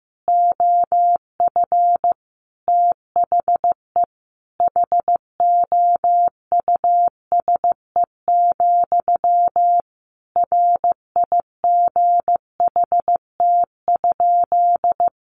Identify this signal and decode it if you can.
Morse code, heard as a keyed tone, about 15 wpm, 700 Hz. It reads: OF THE HOUSE, RIGHT?
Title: Sentences from Top 1200 Words 15wpm